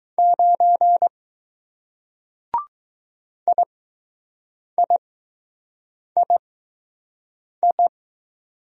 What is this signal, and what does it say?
Morse code, heard as a keyed tone, about 20 wpm, spaced as for 6 wpm, 700 Hz. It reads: 9 IIII